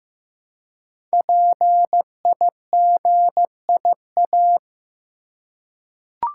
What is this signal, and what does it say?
Morse code, heard as a keyed tone, about 15 wpm, 700 Hz. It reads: PIGIA
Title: Code Groups - 5 Letters - 15wpm